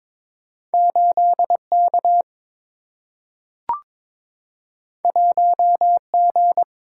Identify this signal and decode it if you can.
Morse code, heard as a keyed tone, about 22 wpm, 700 Hz. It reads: 8 K 1 G
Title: Single-Letter Single-Number 22wpm